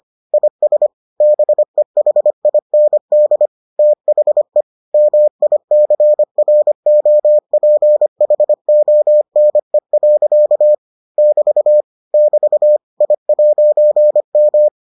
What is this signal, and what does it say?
Morse code, heard as a keyed tone, about 25 wpm, 600 Hz. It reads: IS BEHIND THE MICROPHONE. <BT> <BT> I'M